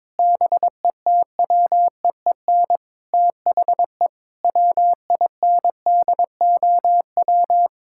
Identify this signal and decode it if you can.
Morse code, heard as a keyed tone, about 21 wpm, 700 Hz. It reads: BETWEEN THE WINDOW
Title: Sentences from Top 1600 Words 22wpm